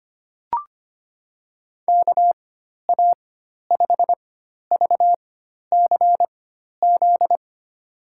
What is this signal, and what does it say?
Morse code, heard as a keyed tone, about 25 wpm, spaced as for 12 wpm, 700 Hz. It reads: KA5VCZ
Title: USA Call Signs - 4x Character Spacing - 25wpm